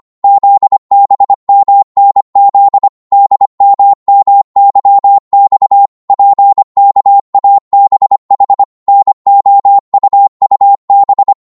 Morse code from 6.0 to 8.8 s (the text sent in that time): PKABH